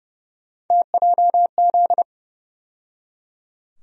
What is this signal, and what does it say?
Morse code, heard as a keyed tone, about 30 wpm, 700 Hz. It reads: TJZ
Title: USA Call Sign Postfixes - 3 Letters 30wpm